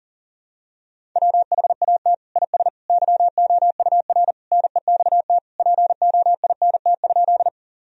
Morse code, heard as a keyed tone, about 37 wpm, 700 Hz. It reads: WHAT IS YOUR NEXT POINT?
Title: Sentences from Top 1000 Words - Review Part 1 - 40wpm